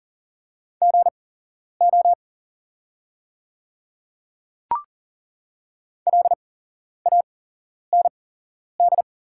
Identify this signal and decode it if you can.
Morse code, heard as a keyed tone, about 40 wpm, spaced as for 11 wpm, 690 Hz. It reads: GO LAND
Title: 8x Character Spacing - Top 500 Words - 40wpm